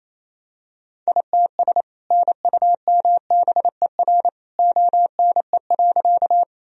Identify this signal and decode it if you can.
Morse code, heard as a keyed tone, about 28 wpm, 700 Hz. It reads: ITS NUMBER ONE.
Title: Sentences from Top 1000 Words - Review Part 1 - 28wpm